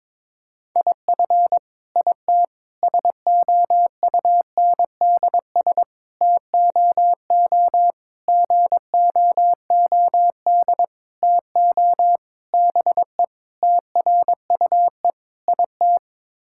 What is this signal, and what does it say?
Morse code, heard as a keyed tone, about 22 wpm, 700 Hz. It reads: IF IT SOUNDS TOO GOOD TO BE TRUE IT
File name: Sets of 9 Words - 22wpm